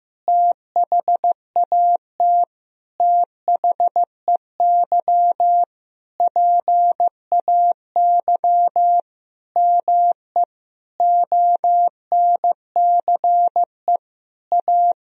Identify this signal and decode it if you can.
Morse code, heard as a keyed tone, about 16 wpm, 700 Hz. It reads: THAT THEY PAY ME ONCE A
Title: Sentences from Top 1000 Words - Review Part 1 - 15wpm